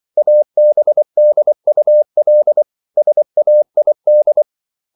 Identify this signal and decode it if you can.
Morse code, heard as a keyed tone, about 24 wpm, 600 Hz. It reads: ABDUL SAID